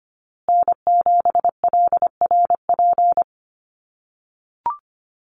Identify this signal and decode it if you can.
Morse code, heard as a keyed tone, about 25 wpm, 700 Hz. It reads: N7LRP
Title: USA Call Signs Repeated 2x 25wpm